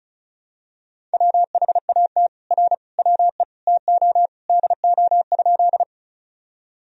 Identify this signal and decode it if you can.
Morse code, heard as a keyed tone, about 36 wpm, 700 Hz. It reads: WHAT R WE TO DO?
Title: Sentences from Top 100 Words and Abbreviations 35wpm